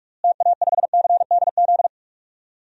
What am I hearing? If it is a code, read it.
Morse code, heard as a keyed tone, about 46 wpm, 700 Hz. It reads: TA5CDZ
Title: DXCC-100 Call Signs - 6 Characters - Repeated 2x - 45wpm